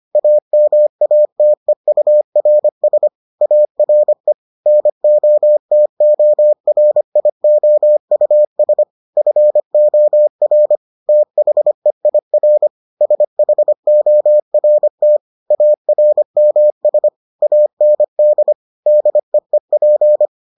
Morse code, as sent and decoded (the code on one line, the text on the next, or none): .- -- .- - . ..- .-. ... / .- .-. / -. --- - --- .-. .. --- ..- ... / ..-. --- .-. / - .... .. .-. / ... .... --- .-. - / .- .-. -- ... / .- -. -.. / -.. . .--.
AMATEURS ARE NOTORIOUS FOR THEIR SHORT ARMS AND DEEP